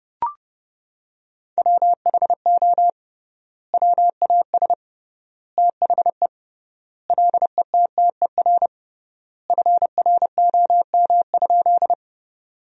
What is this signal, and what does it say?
Morse code, heard as a keyed tone, about 30 wpm, 700 Hz. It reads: WHO WAS THE LETTER FROM?
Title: 3x Word-Spacing - Sentences from Top 300 Words - 30wpm